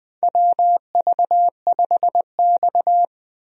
Morse code, heard as a keyed tone, about 20 wpm, 700 Hz. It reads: WV5X
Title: USA Call Signs - 4 Characters - 20wpm